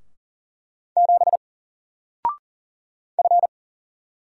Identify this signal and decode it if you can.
Morse code, heard as a keyed tone, about 39 wpm, 700 Hz. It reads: 7 F
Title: Single Letter-Number 40wpm